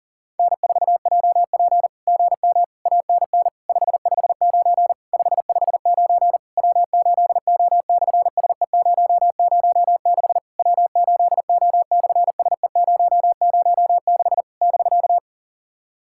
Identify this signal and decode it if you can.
Morse code, heard as a keyed tone, about 37 wpm, 700 Hz. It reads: N4JP GM ANN 559 559 W8O/SE006 W8O/SE006 <BK>
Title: SOTA QSO Practice 40wpm